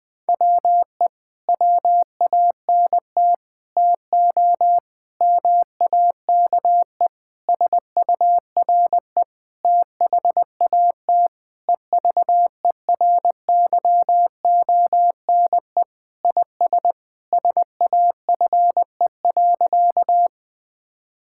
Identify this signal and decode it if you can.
Morse code, heard as a keyed tone, about 20 wpm, 700 Hz. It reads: WE WANT TO MAKE SURE THAT EVERYONE IS SAFE.
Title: Sentences with 5-Word Phrases - 20wpm